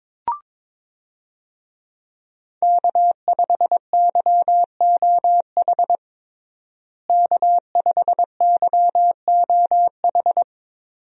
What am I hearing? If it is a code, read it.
Morse code, heard as a keyed tone, about 22 wpm, 700 Hz. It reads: K5YOH K5YOH
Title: USA Call Signs - 5 Characters - Repeated 2x - 22wpm